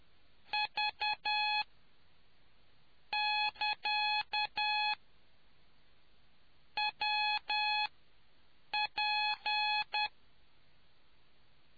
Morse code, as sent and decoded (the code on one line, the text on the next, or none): ...- / -.-.- / .-- / .--.
V <KA> W P